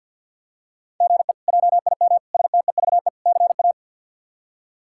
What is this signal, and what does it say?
Morse code, heard as a keyed tone, about 51 wpm, 690 Hz. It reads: GE JIM STEVE CA
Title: K1USN SST QSO Practice - 50wpm